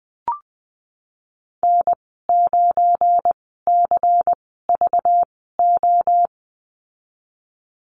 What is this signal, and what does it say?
Morse code, heard as a keyed tone, about 20 wpm, spaced as for 15 wpm, 700 Hz. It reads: N9CVO
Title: 238 Lesson 38g - Letter q - Call Signs 20wpm